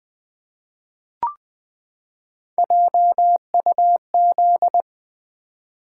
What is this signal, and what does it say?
Morse code, heard as a keyed tone, about 20 wpm, 700 Hz. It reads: JUZ
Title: Three Letter-Number Combinations - 20wpm